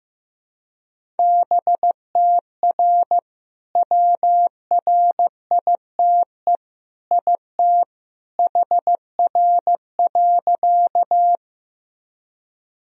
Morse code, heard as a keyed tone, about 15 wpm, 700 Hz. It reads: BTR WRITE IT HR.
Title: Sentences from Top 400 Words and Abbreviations 15wpm